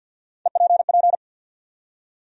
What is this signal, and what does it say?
Morse code, heard as a keyed tone, about 53 wpm, 700 Hz. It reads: EPP
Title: USA Call Sign Postfixes - 3 Letters 50wpm